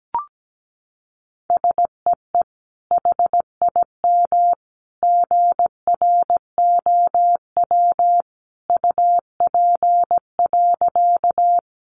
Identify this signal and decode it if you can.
Morse code, heard as a keyed tone, about 17 wpm, 700 Hz. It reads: SEE HIM GROW UP.
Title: Sentences from Top 300 Words 17wpm